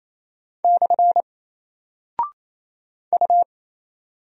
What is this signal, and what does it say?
Morse code, heard as a keyed tone, about 28 wpm, 700 Hz. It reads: / U
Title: Single Character (Letter:Number:Punctuation:Prosign) 28wpm